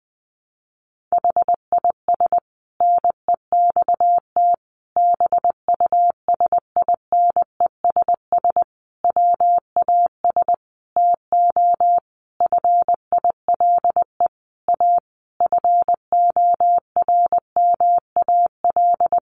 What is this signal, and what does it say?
Morse code, heard as a keyed tone, about 20 wpm, 700 Hz. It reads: HIS NEXT BUSINESS WAS TO FILE A FORMAL
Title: Sentences from Top 1300 Words 20wpm